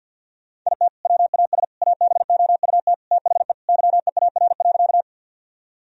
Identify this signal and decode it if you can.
Morse code, heard as a keyed tone, about 51 wpm, 700 Hz. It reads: IT WAS ABOUT THE YEAR.
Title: Sentences from Top 1000 Words - Review Part 1 - 50wpm